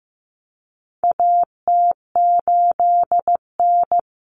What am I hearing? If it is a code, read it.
Morse code, heard as a keyed tone, about 15 wpm, 700 Hz. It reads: AT8N
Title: Common Cut Numbers - 4-Digit 15wpm